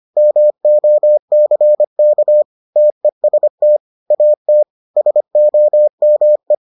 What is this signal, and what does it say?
Morse code, heard as a keyed tone, about 25 wpm, 600 Hz. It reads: MOCK TEST AT SOME